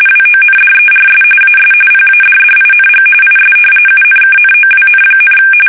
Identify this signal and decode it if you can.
RTTY, about 45.45 baud. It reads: WONDERS AS PADDING IS UNKNOWN,
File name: fsk850 / battleshipnj